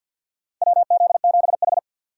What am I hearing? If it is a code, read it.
Morse code, heard as a keyed tone, about 52 wpm, 700 Hz. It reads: WZ7H
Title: USA Call Signs - 4x Character Spacing - 50wpm